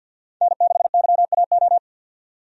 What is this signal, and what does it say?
Morse code, heard as a keyed tone, about 49 wpm, 700 Hz. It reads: N6YAO
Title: USA Call Signs - 5 Characters - 50wpm